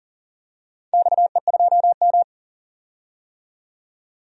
Morse code, heard as a keyed tone, about 40 wpm, 700 Hz. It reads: XE2M
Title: DXCC-100 Call Signs - 4 Characters - 40wpm